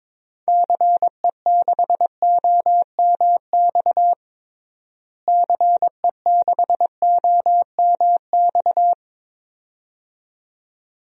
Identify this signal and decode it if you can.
Morse code, heard as a keyed tone, about 22 wpm, 700 Hz. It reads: CE6OMX CE6OMX
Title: DXCC-100 Call Signs - 6 Characters - Repeated 2x - 22wpm